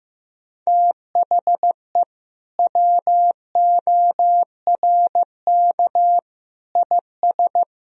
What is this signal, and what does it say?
Morse code, heard as a keyed tone, about 15 wpm, 700 Hz. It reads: THE WORK IS
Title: Sentences from Top 1000 Words - Review Part 4 - 15wpm